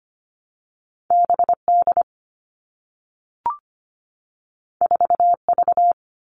Morse code, heard as a keyed tone, about 25 wpm, 700 Hz. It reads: BD 4V
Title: All Cut Numbers - 2-Digit 25wpm